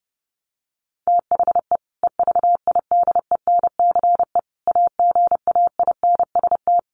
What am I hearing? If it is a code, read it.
Morse code, heard as a keyed tone, about 30 wpm, 700 Hz. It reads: THE EVIDENCE AGAINST
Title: Sentences from Top 1300 Words 30wpm